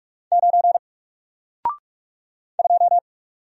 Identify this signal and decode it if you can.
Morse code, heard as a keyed tone, about 44 wpm, 700 Hz. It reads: 9 2
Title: Single Letter-Number 45wpm